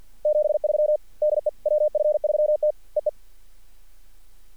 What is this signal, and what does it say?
Morse code, heard as a keyed tone, about 46 wpm, 600 Hz. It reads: 73 DE WF3T EE
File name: archived / CRSnet